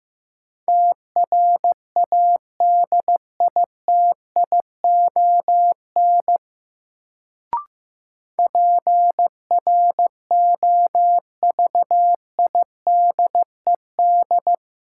Morse code, heard as a keyed tone, about 15 wpm, 700 Hz. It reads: TRADITION PROVIDED
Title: Sentences from Top 1500 Words - New Word Drill - 15wpm